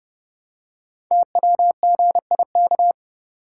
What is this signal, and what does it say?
Morse code, heard as a keyed tone, about 30 wpm, 700 Hz. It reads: TWGIK
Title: Code Groups - 5 Letters - 30wpm